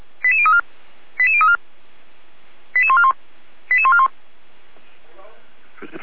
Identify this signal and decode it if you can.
Five-tone selective call (ZVEI), heard as quick runs of five tones, tones of about 70 ms, with speech. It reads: 88034 88034 88141 88141